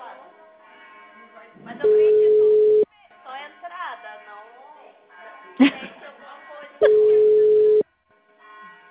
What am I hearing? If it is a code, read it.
Ringback tone (Europe, 425 Hz), with speech.